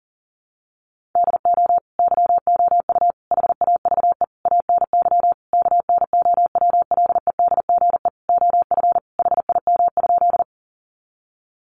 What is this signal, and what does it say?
Morse code, heard as a keyed tone, about 37 wpm, 700 Hz. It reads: DO YOU HAVE ANY KNOWLEDGE OF HIM?
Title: Sentences from Top 600 Words 40wpm